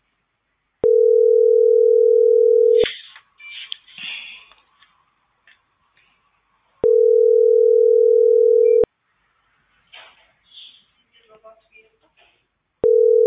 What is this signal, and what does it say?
Ringback tone (North America), 440 + 480 Hz.